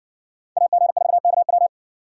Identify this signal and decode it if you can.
Morse code, heard as a keyed tone, about 57 wpm, 700 Hz. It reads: AG4KW